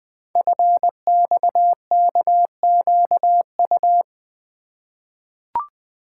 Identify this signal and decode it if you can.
Morse code, heard as a keyed tone, about 20 wpm, 700 Hz. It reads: FXKQU
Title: Code Groups - 5 Letters - 20wpm